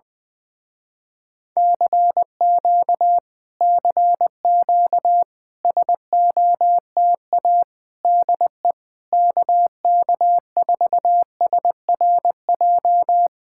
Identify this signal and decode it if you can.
Morse code, heard as a keyed tone, about 20 wpm, 700 Hz. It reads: CQ CQ SOTA DE KK4SRJ
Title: Speed-Racing Common QSO Phrases - 17wpm